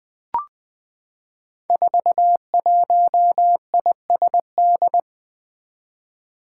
Morse code, heard as a keyed tone, about 20 wpm, 700 Hz. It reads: 41ISD
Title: Five Challenging Letter-Number Combinations 20wpm